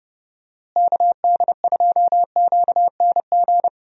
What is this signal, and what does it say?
Morse code, heard as a keyed tone, about 30 wpm, 700 Hz. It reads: KD2QNG
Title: USA Call Signs - 6 Characters - Repeated 3x - 30wpm